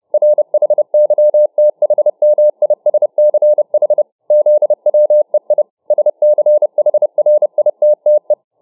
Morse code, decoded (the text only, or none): RHYTHMISCH ZWEI SCHRITTE